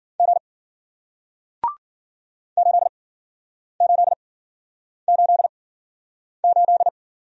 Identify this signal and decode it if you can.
Morse code, heard as a keyed tone, about 52 wpm, spaced as for 9 wpm, 700 Hz. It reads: G 8888